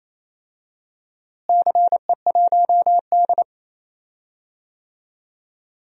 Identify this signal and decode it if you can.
Morse code, heard as a keyed tone, about 28 wpm, 700 Hz. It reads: CE1D
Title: DXCC-100 Call Signs - 4 Characters - Repeated 3x - 28wpm